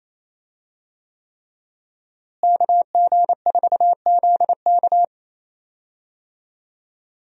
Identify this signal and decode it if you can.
Morse code, heard as a keyed tone, about 28 wpm, 700 Hz. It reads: KG4ZK
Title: USA Call Signs - 5 Characters - Repeated 3x - 28wpm